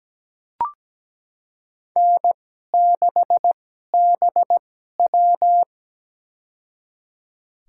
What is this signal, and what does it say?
Morse code, heard as a keyed tone, about 17 wpm, spaced as for 12 wpm, 700 Hz. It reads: N6BW
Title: USA Call Signs - 4 Characters - 2x Character Spacing - 17wpm